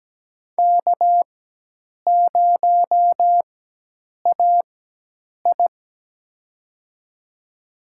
Morse code, heard as a keyed tone, about 17 wpm, spaced as for 8 wpm, 700 Hz. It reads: K0AI
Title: USA Call Signs - 4x Character Spacing - 17wpm